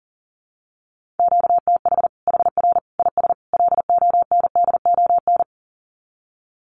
Morse code, heard as a keyed tone, about 40 wpm, 700 Hz. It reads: QTH HR IS LONDON
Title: Common QSO Phrases 40wpm